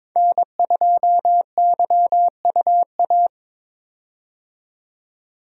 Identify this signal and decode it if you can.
Morse code, heard as a keyed tone, about 22 wpm, 700 Hz. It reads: N2YUA
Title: USA Call Signs - 5 Characters - 22wpm